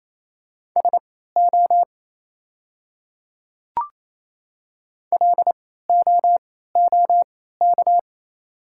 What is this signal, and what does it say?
Morse code, heard as a keyed tone, about 28 wpm, spaced as for 16 wpm, 690 Hz. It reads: SO LOOK
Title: 3x Character Spacing - Top 100 Words - 28wpm